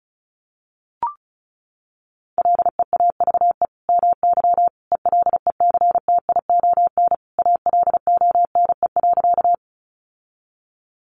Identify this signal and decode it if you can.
Morse code, heard as a keyed tone, about 34 wpm, 700 Hz. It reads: LEAVE MY ELECTION ALONE.
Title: Sentences from Top 1500 Words 35wpm